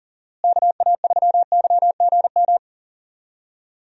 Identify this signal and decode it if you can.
Morse code, heard as a keyed tone, about 40 wpm, 700 Hz. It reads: KA3YGM
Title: USA Call Signs - 6 Characters - Repeated 3x - 40wpm